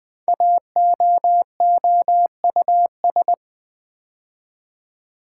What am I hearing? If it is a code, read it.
Morse code, heard as a keyed tone, about 20 wpm, 700 Hz. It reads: AOOUS
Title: Code Groups - 5 Letters - 20wpm